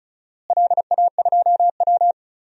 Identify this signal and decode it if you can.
Morse code, heard as a keyed tone, about 34 wpm, 700 Hz. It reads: LA2W
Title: DXCC-100 Call Signs - 4 Characters - Repeated 3x - 35wpm